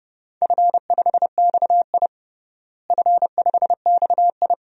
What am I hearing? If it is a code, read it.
Morse code, heard as a keyed tone, about 30 wpm, 700 Hz. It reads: F5XI F5XI
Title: DXCC-100 Call Signs - 4 Characters - Repeated 2x - 30wpm